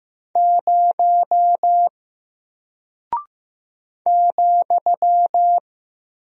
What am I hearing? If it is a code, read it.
Morse code, heard as a keyed tone, about 15 wpm, 700 Hz. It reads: 0 ,
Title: Single Character (Letter:Number:Punctuation:Prosign) Alt 15wpm